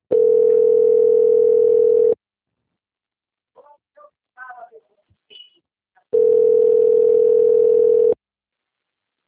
Ringback tone (North America), 440 + 480 Hz.
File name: out- - - - - .20936